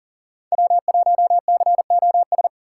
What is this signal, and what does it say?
Morse code, heard as a keyed tone, about 40 wpm, 700 Hz. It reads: W1COS
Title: USA Call Signs - 5 Characters - 6x Character Spacing - 40wpm